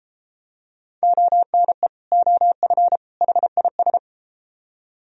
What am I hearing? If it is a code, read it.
Morse code, heard as a keyed tone, about 31 wpm, 700 Hz. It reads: ONE OF HIS